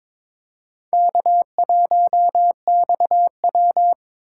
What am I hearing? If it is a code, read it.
Morse code, heard as a keyed tone, about 26 wpm, 700 Hz. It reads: K1XW
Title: Speed Racing - USA 1x2 Call Signs 22wpm